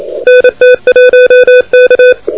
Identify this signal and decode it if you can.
Morse code, heard as a keyed tone, about 33 wpm, 510 Hz, with noise. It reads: NT1K
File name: recordings / F6HKA de NT1K ARRL10M CW 28MHz